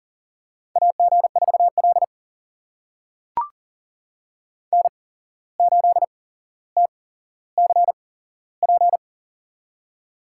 Morse code, heard as a keyed tone, about 40 wpm, spaced as for 16 wpm, 700 Hz. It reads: AG4L N8TCP
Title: USA Call Signs - 8x Character Spacing - 40wpm